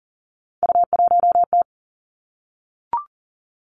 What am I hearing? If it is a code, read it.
Morse code, heard as a keyed tone, about 38 wpm, 700 Hz. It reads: U1T
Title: Three Challenging Letter-Number Combinations 40wpm